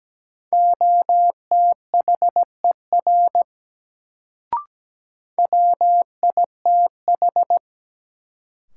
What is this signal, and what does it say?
Morse code, heard as a keyed tone, about 17 wpm, 700 Hz. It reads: OTHER WITH
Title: Sets of 2 Words - New Word Drill - 17wpm